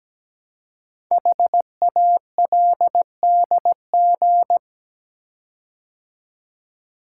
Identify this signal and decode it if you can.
Morse code, heard as a keyed tone, about 17 wpm, 700 Hz. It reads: HALDG
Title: Code Groups - 5 Letters - 17wpm